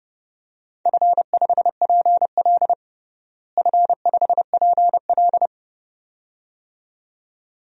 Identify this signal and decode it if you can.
Morse code, heard as a keyed tone, about 32 wpm, 700 Hz. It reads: F5PL F5PL
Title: DXCC-100 Call Signs - 4 Characters - Repeated 2x - 30wpm